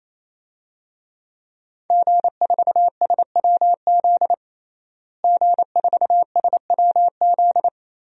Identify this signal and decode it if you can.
Morse code, heard as a keyed tone, about 28 wpm, 700 Hz. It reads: G4SWZ G4SWZ
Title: DXCC-100 Call Signs - 5 Characters - Repeated 3x - 28wpm